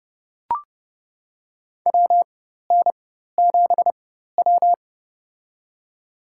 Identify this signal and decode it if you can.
Morse code, heard as a keyed tone, about 30 wpm, spaced as for 14 wpm, 700 Hz. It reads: WN7W
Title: USA Call Signs - 4 Characters - 4x Character Spacing - 30wpm